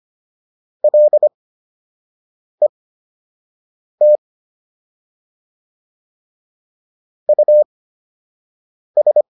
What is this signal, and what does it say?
Morse code, heard as a keyed tone, about 25 wpm, spaced as for 6 wpm, 600 Hz. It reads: LET US